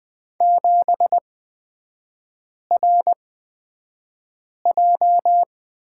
Morse code, heard as a keyed tone, about 20 wpm, spaced as for 5 wpm, 700 Hz. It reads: 7RJ
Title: 250 Lesson 40e - BK - Three Characters 20wpm